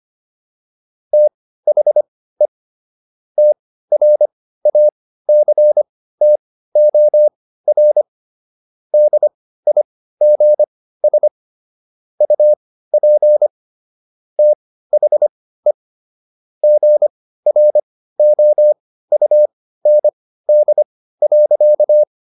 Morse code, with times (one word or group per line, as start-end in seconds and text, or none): 1.1-2.5 THE
3.4-8.0 TRACTOR
8.9-11.3 DIGS
12.2-13.5 UP
14.4-15.7 THE
16.6-22.0 GROUND.